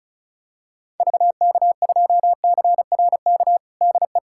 Morse code, heard as a keyed tone, about 36 wpm, 700 Hz. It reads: VK2CRK DE
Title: USA Callsign DE Callsign 35wpm